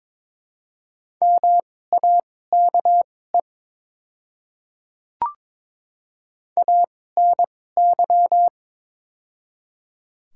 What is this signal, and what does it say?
Morse code, heard as a keyed tone, about 22 wpm, spaced as for 16 wpm, 700 Hz. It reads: MAKE ANY
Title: 2x Character Spacing - Top 200 Words - 22wpm